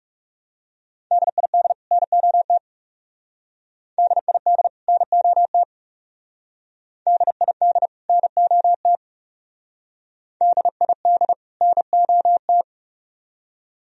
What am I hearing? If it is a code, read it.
Morse code, heard as a keyed tone, about 37 wpm, 700 Hz. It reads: DID NOT DID NOT DID NOT DID NOT
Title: Speed-Racing - Sets of 2 Words 30wpm